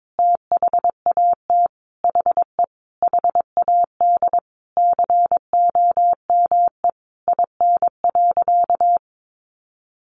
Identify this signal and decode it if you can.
Morse code, heard as a keyed tone, about 22 wpm, 700 Hz. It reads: THAT HE HAD COME IN.
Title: Sentences from Top 1000 Words 22wpm